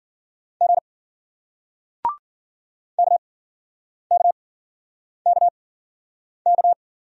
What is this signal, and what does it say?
Morse code, heard as a keyed tone, about 56 wpm, spaced as for 9 wpm, 700 Hz. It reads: G KKKK